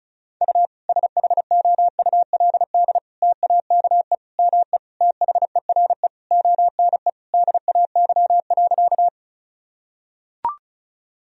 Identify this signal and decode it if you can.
Morse code, heard as a keyed tone, about 36 wpm, 700 Hz. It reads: U SHOULD TAKE ME THERE ONE DAY.
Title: Sentences from Top 300 Words and Abbreviations 35wpm